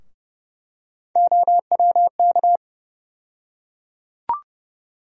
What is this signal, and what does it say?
Morse code, heard as a keyed tone, about 30 wpm, 700 Hz. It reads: OWK
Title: USA Call Sign Postfixes - 3 Letters 30wpm